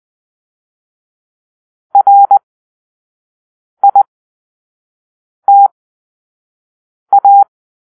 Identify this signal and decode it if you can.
Morse code, heard as a keyed tone, about 20 wpm, spaced as for 5 wpm, 800 Hz. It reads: RITA
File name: arrl / Grp-2 Drill-4